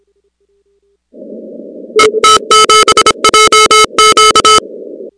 Morse code, heard as a keyed tone, about 26 wpm, 1300 Hz, with noise. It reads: ET7JQ
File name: contests / 2019-cw-wpx-kt5j-k5pi- -2330 0-37